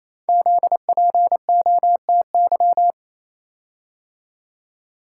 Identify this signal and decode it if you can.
Morse code, heard as a keyed tone, about 29 wpm, 700 Hz. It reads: ZPOTY
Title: Code Groups - 5 Letters - 28wpm